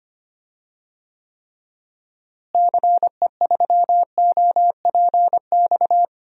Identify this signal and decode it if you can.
Morse code, heard as a keyed tone, about 25 wpm, 700 Hz. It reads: CE3OPX
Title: DXCC-100 Call Signs - 6 Characters - Repeated 2x - 25wpm